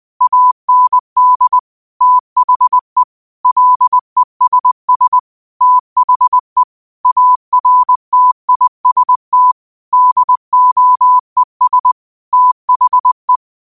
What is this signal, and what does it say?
Morse code, heard as a keyed tone, about 20 wpm, 1000 Hz. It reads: AND THE LESS THE ARTIST DOES THE